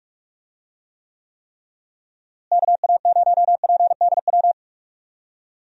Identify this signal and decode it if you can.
Morse code, heard as a keyed tone, about 46 wpm, 700 Hz. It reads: KA0PDW
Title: USA Call Signs - 6 Characters - Repeated 3x - 45wpm